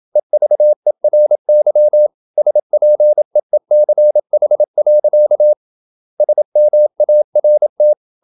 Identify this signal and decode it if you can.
Morse code, heard as a keyed tone, about 28 wpm, 600 Hz. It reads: EVERY SPEECH. SMART
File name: Never 0027wpm